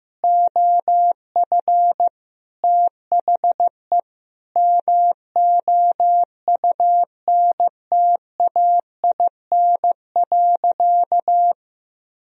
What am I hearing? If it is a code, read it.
Morse code, heard as a keyed tone, about 15 wpm, 700 Hz. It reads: OF THE MOUNTAIN.
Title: Sentences from Top 1000 Words - Review Part 1 - 15wpm